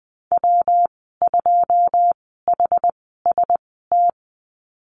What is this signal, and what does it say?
Morse code, heard as a keyed tone, about 20 wpm, spaced as for 15 wpm, 700 Hz. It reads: W2HST
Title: 128 Lesson 21g - Number 6 - Call Signs 20wpm